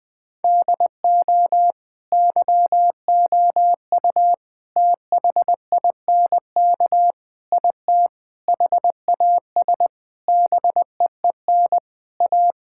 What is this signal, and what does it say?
Morse code, heard as a keyed tone, about 21 wpm, 700 Hz. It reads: DO YOU THINK IT HAS BEEN A
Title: Sentences from Top 1000 Words - Review Part 6 - 20wpm